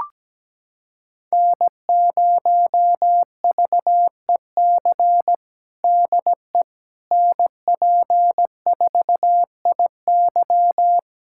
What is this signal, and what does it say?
Morse code, heard as a keyed tone, about 17 wpm, 700 Hz. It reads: N0VEC DE NP4IY
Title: USA Callsign DE Callsign - 5 Characters - 17wpm